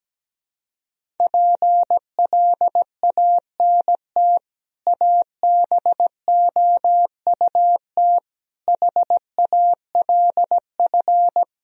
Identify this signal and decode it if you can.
Morse code, heard as a keyed tone, about 17 wpm, 700 Hz. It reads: PLANT ABOUT HALF
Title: Sentences from Top 1000 Words - Review Part 1 - 17wpm